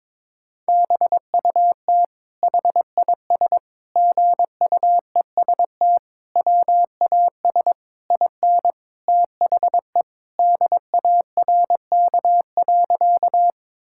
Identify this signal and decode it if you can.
Morse code, heard as a keyed tone, about 22 wpm, 700 Hz. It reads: BUT HIS GUEST WAS IN THE DARK.
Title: Sentences from Top 1600 Words 22wpm